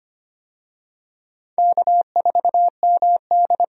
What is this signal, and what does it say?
Morse code, heard as a keyed tone, about 25 wpm, 700 Hz. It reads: K4MD
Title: USA Call Signs - 4 Characters - 25wpm